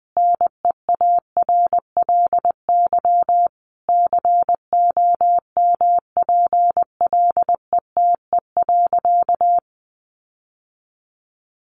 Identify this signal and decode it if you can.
Morse code, heard as a keyed tone, about 20 wpm, 700 Hz. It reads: NEARLY COMPLETE.
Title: Sentences from Top 1300 Words 20wpm